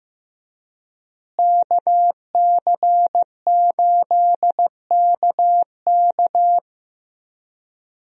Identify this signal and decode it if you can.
Morse code, heard as a keyed tone, about 15 wpm, 700 Hz. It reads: KC8KK
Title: USA Call Signs - 5 Characters - Repeated 2x - 15wpm